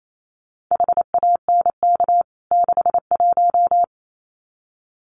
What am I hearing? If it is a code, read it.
Morse code, heard as a keyed tone, about 28 wpm, 700 Hz. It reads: HANK 61
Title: CWT Exchange 28wpm